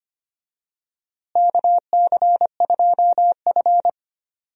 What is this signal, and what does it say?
Morse code, heard as a keyed tone, about 25 wpm, 700 Hz. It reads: KC2F
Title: USA Call Signs - 4 Characters - Repeated 2x - 25wpm